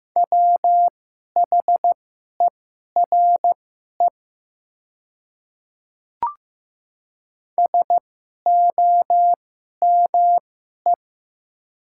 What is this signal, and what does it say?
Morse code, heard as a keyed tone, about 15 wpm, spaced as for 11 wpm, 700 Hz. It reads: WHERE SOME
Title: 2x Character Spacing - Top 200 Words - 15wpm